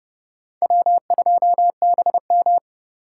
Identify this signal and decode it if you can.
Morse code, heard as a keyed tone, about 25 wpm, 700 Hz. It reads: W2BM